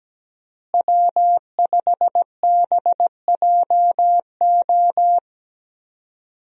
Morse code, heard as a keyed tone, about 17 wpm, 700 Hz. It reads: W5BJO
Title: USA Call Signs - 5 Characters - Repeated 3x - 17wpm